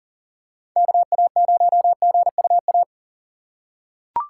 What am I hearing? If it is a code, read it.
Morse code, heard as a keyed tone, about 37 wpm, 700 Hz. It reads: KA0GUA
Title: USA Call Signs - 6 Characters - 2x Character Spacing - 40wpm